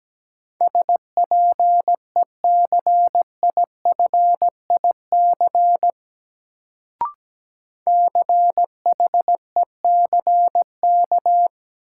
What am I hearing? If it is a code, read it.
Morse code, heard as a keyed tone, about 17 wpm, 700 Hz. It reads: SPECIFIC CHECK
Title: Sentences from Top 1000 Words - Review Part 3 - New Word Drill - 17wpm